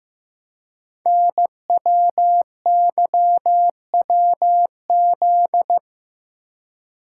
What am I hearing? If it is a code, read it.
Morse code, heard as a keyed tone, about 15 wpm, 700 Hz. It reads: NWYWZ
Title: Code Groups - 5 Letters - 15wpm